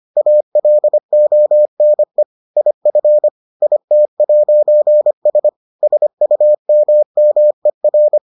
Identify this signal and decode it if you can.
Morse code, heard as a keyed tone, about 25 wpm, 600 Hz. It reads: ALONE IF IT'S SUMMER